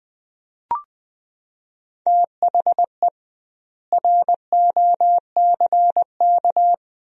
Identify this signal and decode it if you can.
Morse code, heard as a keyed tone, about 20 wpm, 700 Hz. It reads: THE ROCK
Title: 2x Word-Spacing - Sentences from Top 400 Words - 20wpm